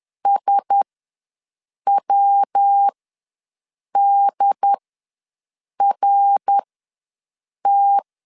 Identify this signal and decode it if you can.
Morse code, heard as a keyed tone, about 10 wpm, spaced as for 6 wpm, 800 Hz. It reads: SWDRT
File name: LEZIONI / 5 LBVC / 5 LBVC ws 10wpm 3733